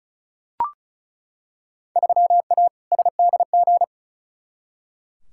Morse code, heard as a keyed tone, about 36 wpm, 700 Hz. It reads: 3A SDG
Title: Field Day Exchanges - 35wpm